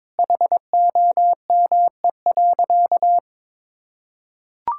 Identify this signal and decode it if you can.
Morse code, heard as a keyed tone, about 22 wpm, 700 Hz. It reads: HOME.
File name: practice / 22wpm / Sentences with 4-Word Phrases - 22wpm